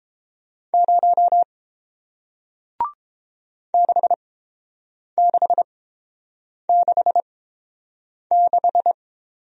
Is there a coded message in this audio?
Yes: Morse code from 0.7 to 8.9 s, heard as a keyed tone, about 30 wpm, spaced as for 7 wpm, 700 Hz. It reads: 0 6666